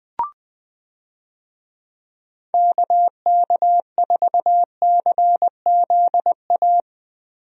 Morse code, heard as a keyed tone, about 20 wpm, 700 Hz. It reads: KK4CZA
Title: USA Call Signs - 6 Characters - Repeated 3x - 20wpm